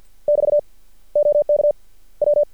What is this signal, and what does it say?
Morse code, heard as a keyed tone, about 50 wpm, 590 Hz. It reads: <BT> OK W
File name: archived / CRSnet